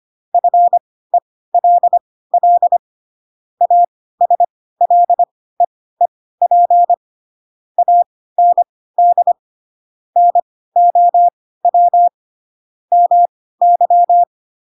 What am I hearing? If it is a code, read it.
Morse code, heard as a keyed tone, about 25 wpm, spaced as for 16 wpm, 700 Hz. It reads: FELL ASLEEP AND NOW MY
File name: Pinocchio / Pinocchio ch7 10 25 ik1hge